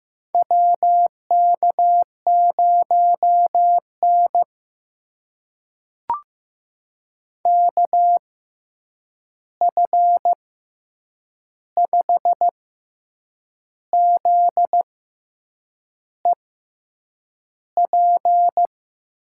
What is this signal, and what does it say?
Morse code, heard as a keyed tone, about 15 wpm, spaced as for 7 wpm, 700 Hz. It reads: WK0N KF5ZEP
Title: USA Call Signs - 6x Character Spacing - 15wpm